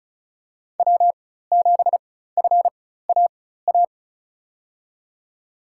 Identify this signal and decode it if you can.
Morse code, heard as a keyed tone, about 34 wpm, spaced as for 16 wpm, 700 Hz. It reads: W7FAA
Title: USA Call Signs - 5 Characters - 4x Character Spacing - 35wpm